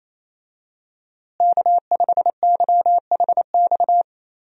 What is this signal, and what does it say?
Morse code, heard as a keyed tone, about 28 wpm, 700 Hz. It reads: K5YHX
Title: USA Call Signs - 5 Characters - Repeated 3x - 28wpm